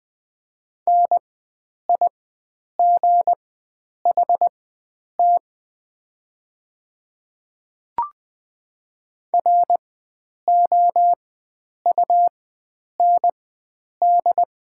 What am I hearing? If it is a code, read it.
Morse code, heard as a keyed tone, about 20 wpm, spaced as for 9 wpm, 700 Hz. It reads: NIGHT ROUND
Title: 4x Character Spacing - Top 300 Words - 20wpm